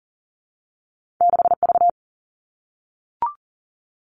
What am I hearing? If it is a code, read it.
Morse code, heard as a keyed tone, about 38 wpm, 700 Hz. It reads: 6 V
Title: Two Challenging Letter-Number Combinations 40wpm